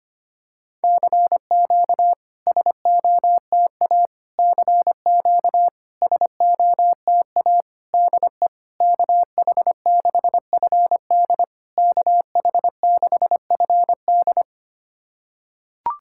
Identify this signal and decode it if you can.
Morse code, heard as a keyed tone, about 25 wpm, 700 Hz. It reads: CQ SOTA CQ SOTA DE KH6FD KH6FD
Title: SOTA QSO Practice 25wpm